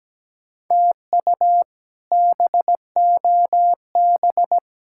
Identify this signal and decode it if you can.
Morse code, heard as a keyed tone, about 17 wpm, 700 Hz. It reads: TU BOB
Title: K1USN SST QSO Practice - 17wpm